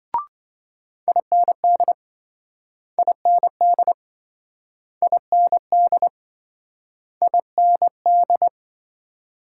Morse code, heard as a keyed tone, about 27 wpm, 700 Hz. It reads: IND IND IND IND